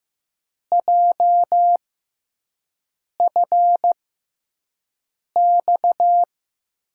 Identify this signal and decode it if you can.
Morse code, heard as a keyed tone, about 15 wpm, spaced as for 5 wpm, 700 Hz. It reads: JFX
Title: USA Call Signs - 6 Characters - 6x Character Spacing - 15wpm